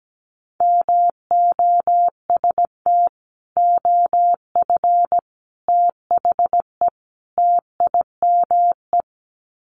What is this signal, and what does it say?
Morse code, heard as a keyed tone, about 17 wpm, 700 Hz. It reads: MOST OF THE TIME
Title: Sets of 4 Words 17wpm